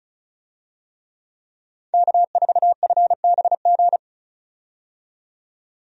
Morse code, heard as a keyed tone, about 36 wpm, 700 Hz. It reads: K4FBG
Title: USA Call Signs - 5 Characters - 35wpm